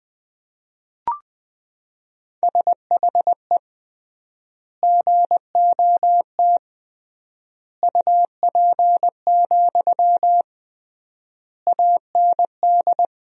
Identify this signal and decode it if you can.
Morse code, heard as a keyed tone, about 21 wpm, 700 Hz. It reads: SHE GOT UP, AND
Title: 3x Word-Spacing - Sentences from Top 500 Words - 20wpm